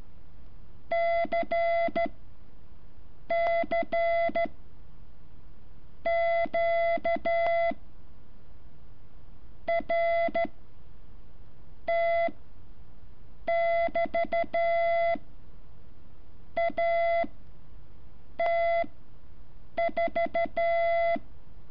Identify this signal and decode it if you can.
Morse code, heard as a keyed tone, about 13 wpm, 680 Hz. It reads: C C Q R T <BT> A T 4
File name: curso / ejercicio 14